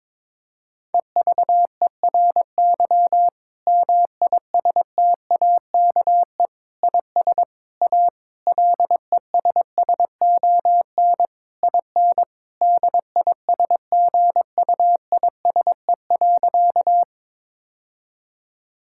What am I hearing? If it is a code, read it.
Morse code, heard as a keyed tone, about 22 wpm, 700 Hz. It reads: EVERY MISTAKE IS A LESSON IN DISGUISE.
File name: Difficult Sentences with Hints - 22wpm